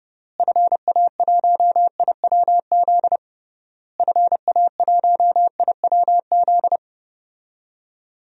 Morse code, heard as a keyed tone, about 30 wpm, 700 Hz. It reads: FA1IWZ FA1IWZ
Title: DXCC-100 Call Signs - 6 Characters - Repeated 3x - 30wpm